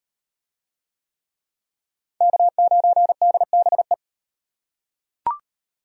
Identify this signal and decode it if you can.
Morse code, heard as a keyed tone, about 36 wpm, 700 Hz. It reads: K9DBE